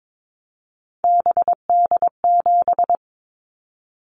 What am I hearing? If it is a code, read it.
Morse code, heard as a keyed tone, about 22 wpm, 700 Hz. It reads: BD7
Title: Callsign Prefixes for Top 100 DXCC - 22wpm